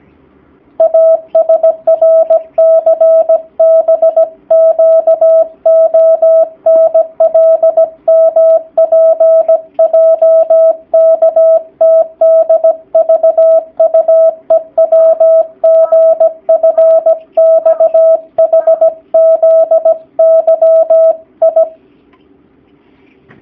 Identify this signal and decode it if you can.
Morse code, heard as a keyed tone, about 17 wpm, 650 Hz. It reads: ASRCBQONLMPJKTDVUEWGFXHZYI
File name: jh8cbh / 2019cw / sp16- w